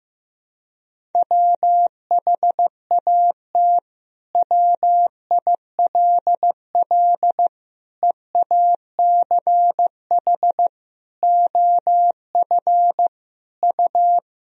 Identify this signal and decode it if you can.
Morse code, heard as a keyed tone, about 15 wpm, 700 Hz. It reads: WHAT WILL EACH OF U
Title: Sentences from Top 200 Words and Abbreviations 15wpm